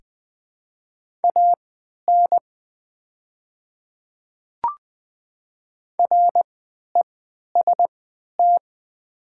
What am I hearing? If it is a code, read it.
Morse code, heard as a keyed tone, about 20 wpm, spaced as for 11 wpm, 700 Hz. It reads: AN REST
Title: 3x Character Spacing - Top 500 Words - 20wpm